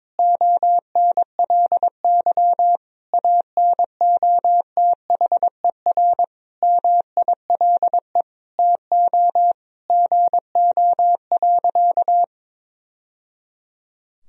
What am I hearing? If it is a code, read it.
Morse code, heard as a keyed tone, about 21 wpm, 700 Hz. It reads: ONLY ANOTHER MILE TO GO.
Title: Sentences from Top 1000 Words - Review Part 1 - 22wpm